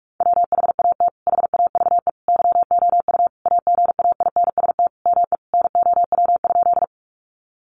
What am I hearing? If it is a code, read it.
Morse code, heard as a keyed tone, about 46 wpm, 700 Hz. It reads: WHAT HAVE YOU AGAINST ME NOW?
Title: Sentences from Top 400 Words 45wpm